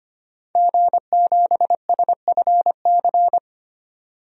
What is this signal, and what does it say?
Morse code, heard as a keyed tone, about 25 wpm, 700 Hz. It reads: G7SFC
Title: DXCC-100 Call Signs - 5 Characters - 25wpm